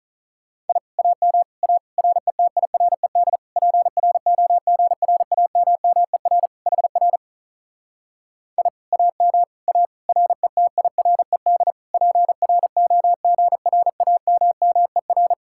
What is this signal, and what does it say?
Morse code, heard as a keyed tone, about 45 wpm, 700 Hz. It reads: I AM A RETIRED PROGRAMMER HR I AM A RETIRED PROGRAMMER